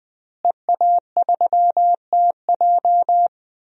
Morse code, heard as a keyed tone, about 21 wpm, 700 Hz. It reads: EA3TJ
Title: DXCC-100 Call Signs - 5 Characters - Repeated 3x - 20wpm